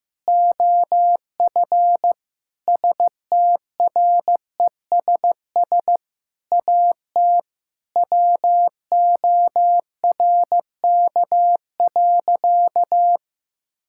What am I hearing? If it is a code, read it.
Morse code, heard as a keyed tone, about 15 wpm, 700 Hz. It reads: OF STRESS AT WORK.
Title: Sentences from Top 1000 Words - Review Part 6 - 15wpm